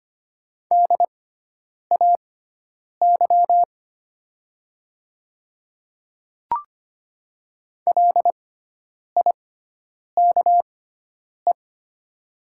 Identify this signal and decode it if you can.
Morse code, heard as a keyed tone, about 25 wpm, spaced as for 9 wpm, 700 Hz. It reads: DAY LIKE
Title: 6x Character Spacing - Top 100 Words - 25wpm